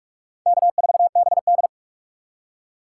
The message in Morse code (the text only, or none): K4BD